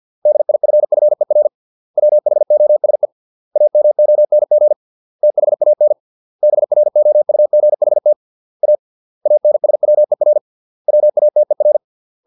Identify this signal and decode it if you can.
Morse code, heard as a keyed tone, about 50 wpm, 600 Hz. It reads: DIFFER WHOSE AMONG THAN BROUGHT A ANSWER WATER